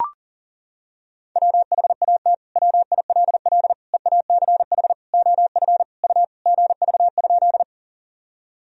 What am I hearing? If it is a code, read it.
Morse code, heard as a keyed tone, about 40 wpm, 700 Hz. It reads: WHAT WILL EACH OF U GV?
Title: Sentences from Top 200 Words and Abbreviations 40wpm